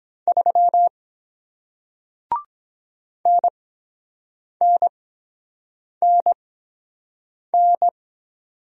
Morse code, heard as a keyed tone, about 23 wpm, spaced as for 7 wpm, 700 Hz. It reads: 3 NNNN